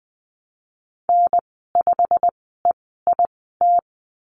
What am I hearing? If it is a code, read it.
Morse code, heard as a keyed tone, about 20 wpm, spaced as for 15 wpm, 700 Hz. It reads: N5EIT
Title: 088 Lesson 15g - Number 5 - Call Signs 20wpm